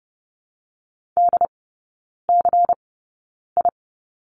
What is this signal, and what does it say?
Morse code, heard as a keyed tone, about 30 wpm, spaced as for 9 wpm, 690 Hz. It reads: DCI
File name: Lesson 16e - Letter c - Three Characters 30wpm